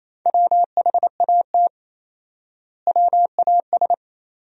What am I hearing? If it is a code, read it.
Morse code, heard as a keyed tone, about 28 wpm, 700 Hz. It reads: WHAT WAS
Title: 4x Word-Spacing - Sentences from Top 300 Words - 28wpm